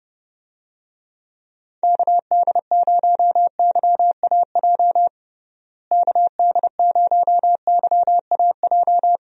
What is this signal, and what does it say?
Morse code, heard as a keyed tone, about 32 wpm, 700 Hz. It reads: KD0YAJ KD0YAJ
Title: USA Call Signs - 6 Characters - Repeated 3x - 30wpm